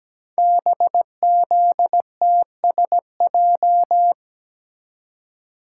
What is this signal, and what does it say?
Morse code, heard as a keyed tone, about 17 wpm, 700 Hz. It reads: BZTSJ
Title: Code Groups - 5 Letters - 17wpm